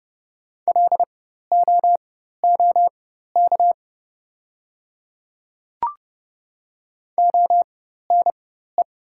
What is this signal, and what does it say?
Morse code, heard as a keyed tone, about 30 wpm, spaced as for 14 wpm, 690 Hz. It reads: LOOK ONE
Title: 4x Character Spacing - Top 100 Words - 30wpm